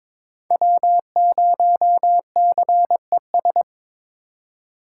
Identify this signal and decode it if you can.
Morse code, heard as a keyed tone, about 26 wpm, 700 Hz. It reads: W0CES
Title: Speed Racing - USA 1x3 Call Signs 22wpm